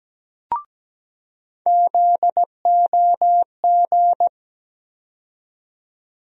Morse code, heard as a keyed tone, about 17 wpm, 700 Hz. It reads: ZOG
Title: USA Call Sign Postfixes - 3 Letters 17wpm